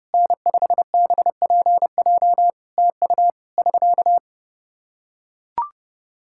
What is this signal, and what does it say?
Morse code, heard as a keyed tone, about 30 wpm, 700 Hz. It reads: N5BPJ TU <SK>
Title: Practice QSO - Exchange 4 - Ending - 30wpm